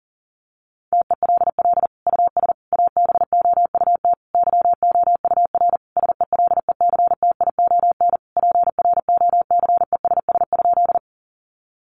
Morse code, heard as a keyed tone, about 40 wpm, 700 Hz. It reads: TELL US ABOUT YOUR SELECTION PROCESS?
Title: Sentences from Top 1400 Words 40wpm